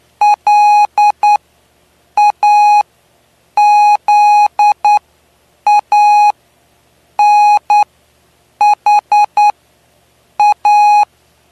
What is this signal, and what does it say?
Morse code, heard as a keyed tone, about 9 wpm, spaced as for 7 wpm, 830 Hz. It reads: LAZANHA
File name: cw / py2lcd / 09-PY2LCD-Li��o 9